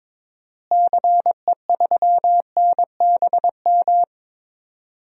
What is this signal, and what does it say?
Morse code, heard as a keyed tone, about 22 wpm, 700 Hz. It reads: CE3NBM
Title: DXCC-100 Call Signs - 6 Characters - Repeated 3x - 22wpm